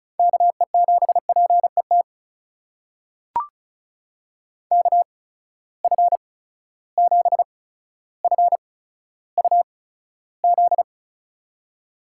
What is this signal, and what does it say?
Morse code, heard as a keyed tone, about 36 wpm, spaced as for 15 wpm, 700 Hz. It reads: KE7PET KF7FUZ
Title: USA Call Signs - 6 Characters - 8x Character Spacing - 35wpm